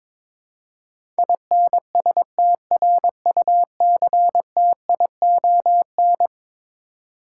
Morse code, heard as a keyed tone, about 22 wpm, 700 Hz. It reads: INSTRUCTION
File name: practice / 22wpm / Eleven Letter Words - 22wpm